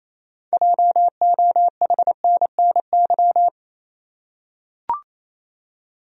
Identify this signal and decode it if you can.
Morse code, heard as a keyed tone, about 28 wpm, 700 Hz. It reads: JOHNNY
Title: USA Common Names - 28wpm